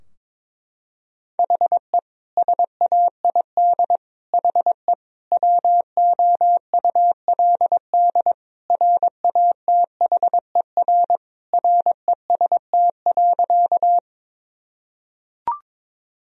Morse code, heard as a keyed tone, about 21 wpm, 700 Hz. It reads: HE SAID HE WOULD RATHER REST.
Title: Sentences from Top 1000 Words - Review Part 4 - 22wpm